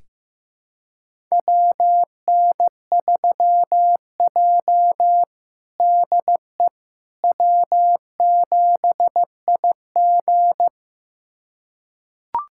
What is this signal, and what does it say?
Morse code, heard as a keyed tone, about 15 wpm, 700 Hz. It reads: WN3J DE W7IG
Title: USA Callsign DE Callsign - 4 Characters - 15wpm